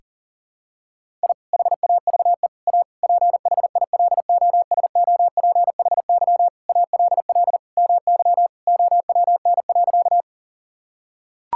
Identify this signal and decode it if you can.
Morse code, heard as a keyed tone, about 40 wpm, 700 Hz. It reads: I HAVE A PHILOSOPHY ALL MY OWN.
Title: Sentences from Top 1000 Words - Review Part 6 - 40wpm